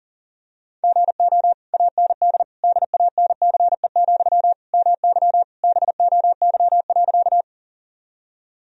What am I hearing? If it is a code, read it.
Morse code, heard as a keyed tone, about 37 wpm, 700 Hz. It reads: GO AND DANCE, MY BOY.
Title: Sentences from Top 1000 Words - Review Part 4 - 40wpm